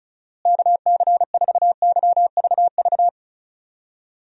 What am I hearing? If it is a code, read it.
Morse code, heard as a keyed tone, about 34 wpm, 700 Hz. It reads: KC4YVV
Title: USA Call Signs - 6 Characters - Repeated 3x - 35wpm